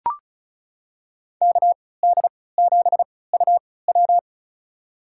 Morse code, heard as a keyed tone, about 36 wpm, spaced as for 20 wpm, 700 Hz. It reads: KD7UW
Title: USA Call Signs - 5 Characters - 3x Character Spacing - 35wpm